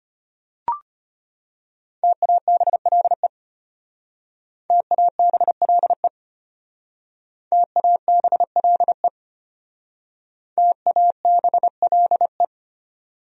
Morse code, heard as a keyed tone, about 30 wpm, 700 Hz. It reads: TABLE TABLE TABLE TABLE
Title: Speed-Racing Top 500 Words 25wpm